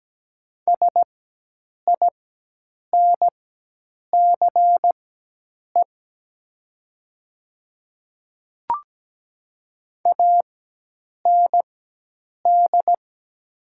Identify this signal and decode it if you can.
Morse code, heard as a keyed tone, about 17 wpm, spaced as for 8 wpm, 700 Hz. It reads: SINCE AND
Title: 4x Character Spacing - Top 300 Words - 17wpm